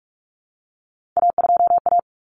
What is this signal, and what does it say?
Morse code, heard as a keyed tone, about 44 wpm, 700 Hz. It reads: A2A
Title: Common Cut Numbers - 3-Digit 45wpm